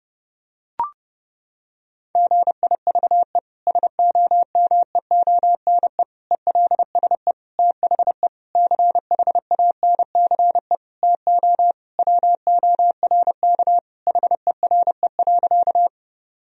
Morse code, heard as a keyed tone, about 30 wpm, 700 Hz. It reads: GIVE SOMEONE ELSE THE CHANCE TO WORK HERE.
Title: Sentences from Top 1000 Words - Review Part 4 - 30wpm